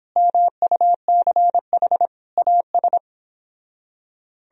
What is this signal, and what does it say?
Morse code, heard as a keyed tone, about 31 wpm, 700 Hz. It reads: MUCH AS